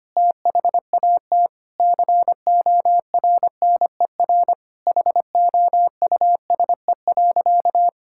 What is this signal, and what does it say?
Morse code, heard as a keyed tone, about 25 wpm, 700 Hz. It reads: THAT CORNER HOUSE.
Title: Sentences from Top 1600 Words 25wpm